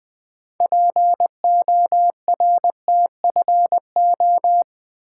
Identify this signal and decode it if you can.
Morse code, heard as a keyed tone, about 20 wpm, 700 Hz. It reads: PORTFO
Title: Build A Word - 3 Letters At A Time - 9-Letter Words - 20wpm